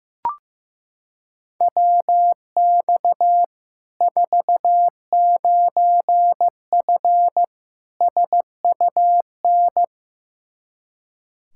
Morse code, heard as a keyed tone, about 15 wpm, 700 Hz. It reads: WX 49F SUN
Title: Practice QSO - Exchange 2 - Info - 15wpm